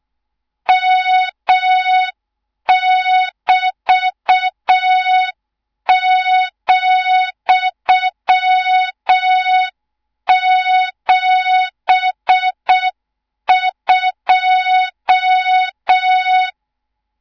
Morse code, heard as a keyed tone, about 6 wpm, 3700 Hz. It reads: M<BT>,72
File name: Morze-HA2MN / hang / 24 GYAKORLAT